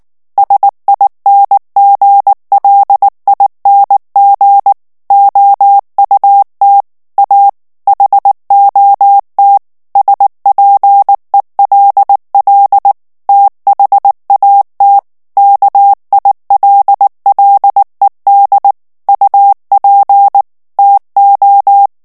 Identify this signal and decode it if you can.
Morse code, heard as a keyed tone, about 19 wpm, 800 Hz. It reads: SINGLING OUT A HOT SPELL THAT KILLED UP TO
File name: morse mp3 / ANGLAIS texte clair 2 vitesse 18 mots minute